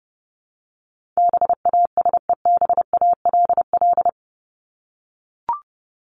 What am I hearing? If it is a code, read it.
Morse code, heard as a keyed tone, about 30 wpm, 700 Hz. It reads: BASEBALL
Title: Top 1751-2000 Words 30wpm